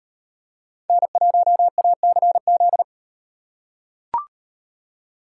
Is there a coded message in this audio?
Yes: Morse code from 0.9 to 2.8 s, heard as a keyed tone, about 37 wpm, 700 Hz. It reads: N1ACZ